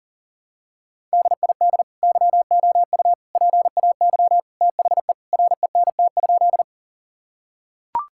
Morse code, heard as a keyed tone, about 37 wpm, 700 Hz. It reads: DID YOU PAY THE RENT?
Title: Sentences from Top 1000 Words - Review Part 5 - 40wpm